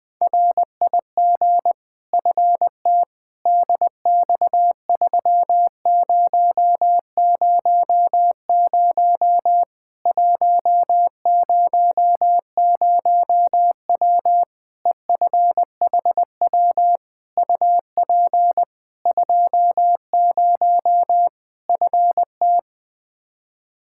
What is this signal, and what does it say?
Morse code, heard as a keyed tone, about 21 wpm, 700 Hz. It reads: RIG FT DX3000 100W EFHW UP 20 FT
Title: K1USN SST QSO Practice 20wpm